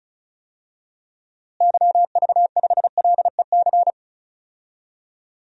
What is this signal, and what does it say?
Morse code, heard as a keyed tone, about 36 wpm, 700 Hz. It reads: YV5LEC
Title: DXCC-100 Call Signs - 6 Characters - Repeated 3x - 35wpm